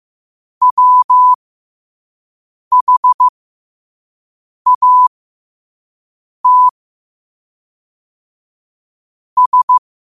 Morse code, heard as a keyed tone, about 15 wpm, spaced as for 5 wpm, 1000 Hz. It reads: WHAT S
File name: qotdfiles / 5wpm